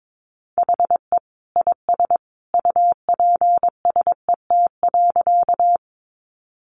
Morse code, heard as a keyed tone, about 22 wpm, 700 Hz. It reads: HE IS UPSET.
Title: Additional Easy Phrases 22wpm